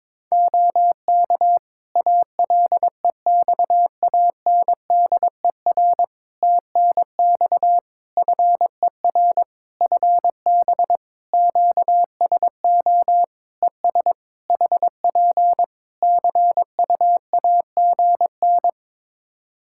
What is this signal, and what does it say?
Morse code, heard as a keyed tone, about 22 wpm, 700 Hz. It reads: OK ALEXANDER TNX FER FB QSO ES HP CUAGN
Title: K1USN SST QSO Practice 22wpm